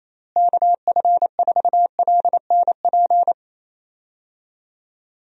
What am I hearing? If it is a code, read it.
Morse code, heard as a keyed tone, about 27 wpm, 700 Hz. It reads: KF4LNP
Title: USA Call Signs - 6 Characters - Repeated 3x - 28wpm